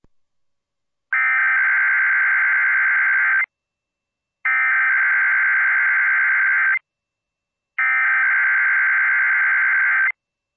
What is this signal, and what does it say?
Emergency Alert System (SAME) tones (header bursts), heard as harsh buzzing bursts. It is heard from 1.1 to 10.1 s.